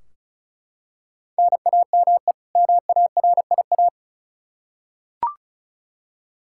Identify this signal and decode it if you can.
Morse code, heard as a keyed tone, about 36 wpm, 700 Hz. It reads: NAME MARIA
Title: Practice QSO - Exchange 1 - Beginning - 35wpm